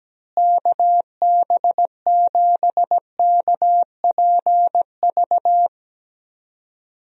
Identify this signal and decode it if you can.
Morse code, heard as a keyed tone, about 17 wpm, 700 Hz. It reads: KB7KPV
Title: USA Call Signs - 6 Characters - Repeated 3x - 17wpm